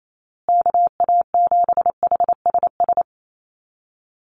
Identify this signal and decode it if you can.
Morse code, heard as a keyed tone, about 28 wpm, 700 Hz. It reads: KA7HSS
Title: USA Call Signs Repeated 2x 28wpm